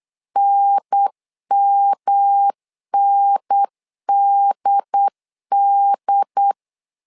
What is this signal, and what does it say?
Morse code, heard as a keyed tone, about 8 wpm, 800 Hz. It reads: NMNDD
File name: LEZIONI / 3 MWRO / 3 M 8wpm 2300